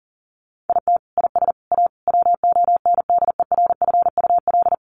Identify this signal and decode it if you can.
Morse code, heard as a keyed tone, about 40 wpm, 700 Hz. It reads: IT IS A WONDERFUL
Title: Sentences from Top 1300 Words 40wpm